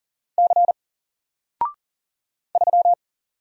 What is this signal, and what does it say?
Morse code, heard as a keyed tone, about 40 wpm, 700 Hz. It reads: C 3
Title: Single Letter-Number 40wpm